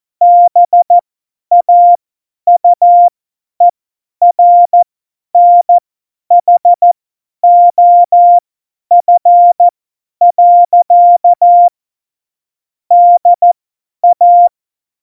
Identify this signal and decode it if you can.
Morse code, heard as a keyed tone, about 14 wpm, spaced as for 10 wpm, 700 Hz. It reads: BAUERNHOF. DA
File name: Morsekurs / Witze / Witz 15. Dezember 2020, 50 F70